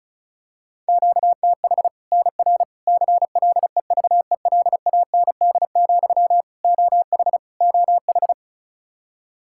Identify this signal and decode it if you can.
Morse code, heard as a keyed tone, about 34 wpm, 700 Hz. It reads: QTH NR CLEVELAND, OH OH
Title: Practice QSO - Exchange 1 - Beginning - 35wpm